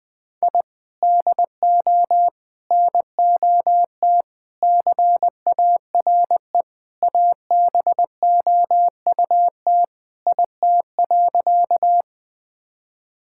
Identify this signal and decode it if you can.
Morse code, heard as a keyed tone, about 20 wpm, 700 Hz. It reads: I DO NOT CARE ABOUT IT.
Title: Sentences with 4-Word Phrases - Spoken Beginning - 20wpm